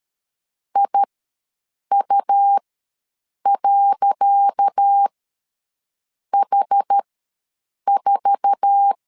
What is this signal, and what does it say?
Morse code, heard as a keyed tone, about 13 wpm, spaced as for 7 wpm, 800 Hz. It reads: IU.H4
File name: LEZIONI / 10 SIGNS / 10C DOT ws 12wpm 3548